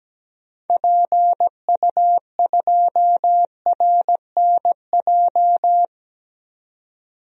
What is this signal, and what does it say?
Morse code, heard as a keyed tone, about 17 wpm, 700 Hz. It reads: PU2RNJ
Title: DXCC-100 Call Signs - 6 Characters - 17wpm